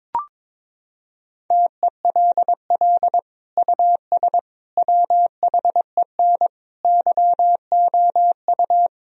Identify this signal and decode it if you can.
Morse code, heard as a keyed tone, about 22 wpm, 700 Hz. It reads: TELL US WHEN YOU
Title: Sentences from Top 1000 Words - Review Part 2 - 22wpm